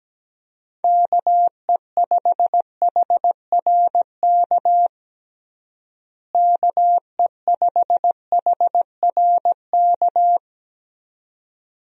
Morse code, heard as a keyed tone, about 17 wpm, 700 Hz. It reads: KE5HRK KE5HRK
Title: USA Call Signs - 6 Characters - Repeated 2x - 17wpm